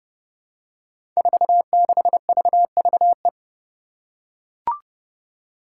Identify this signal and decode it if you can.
Morse code, heard as a keyed tone, about 32 wpm, 700 Hz. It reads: 46VVE
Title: Five Challenging Letter-Number Combinations 30wpm